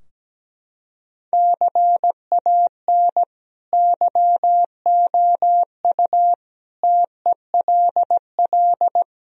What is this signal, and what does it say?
Morse code, heard as a keyed tone, about 17 wpm, 700 Hz. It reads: CAN YOU TELL
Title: Sentences from Top 1000 Words - Review Part 4 - 17wpm